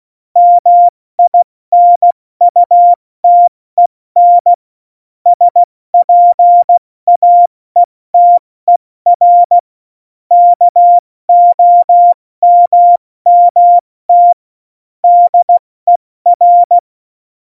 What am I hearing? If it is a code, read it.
Morse code, heard as a keyed tone, about 16 wpm, 700 Hz. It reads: MINUTEN SPAETER KOMMT DER
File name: Morsekurs / Witze / Witz 01. Oktober 2019, 70 F80